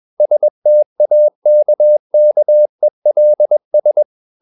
Morse code, heard as a keyed tone, about 21 wpm, 600 Hz. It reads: STAKKELS
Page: Ferie 20 wpm